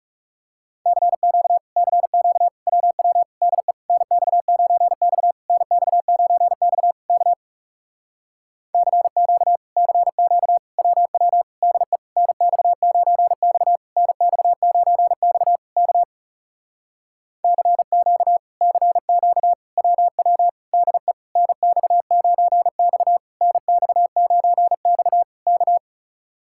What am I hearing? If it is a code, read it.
Morse code, heard as a keyed tone, about 38 wpm, 700 Hz. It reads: CQ CQ WW DE NX9X NX9X K CQ CQ WW DE NX9X NX9X K CQ CQ WW DE NX9X NX9X K
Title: Speed-Racing Common QSO Phrases - 30wpm